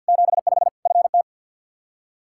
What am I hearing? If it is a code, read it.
Morse code, heard as a keyed tone, about 51 wpm, 700 Hz. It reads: 75 FT
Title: Practice QSO - Exchange 2 - Info - 50wpm